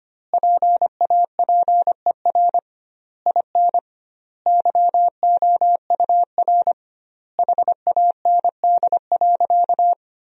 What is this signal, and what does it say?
Morse code, heard as a keyed tone, about 25 wpm, 700 Hz. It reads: PAPER IN YOUR HAND.
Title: 2x Word-Spacing - Sentences from Top 400 Words - 25wpm